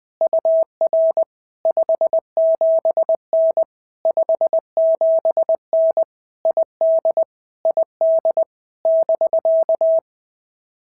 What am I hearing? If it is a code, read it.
Morse code, heard as a keyed tone, about 20 wpm, 650 Hz. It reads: UR 57N 57N ID ID <BK>
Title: POTA QSO Practice - 20wpm